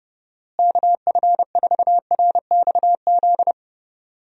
Morse code, heard as a keyed tone, about 30 wpm, 700 Hz. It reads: KF4RXZ
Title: USA Call Signs - 6 Characters - Repeated 3x - 30wpm